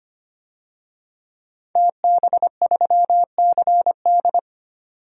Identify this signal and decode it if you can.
Morse code, heard as a keyed tone, about 25 wpm, 700 Hz. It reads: TB3CD
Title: DXCC-100 Call Signs - 5 Characters - 25wpm